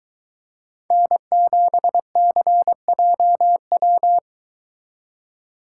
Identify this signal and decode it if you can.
Morse code, heard as a keyed tone, about 22 wpm, 700 Hz. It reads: N7CJW